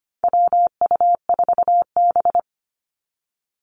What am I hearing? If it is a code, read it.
Morse code, heard as a keyed tone, about 25 wpm, 700 Hz. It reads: WU4B
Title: Call Signs Repeated 2x - 25wpm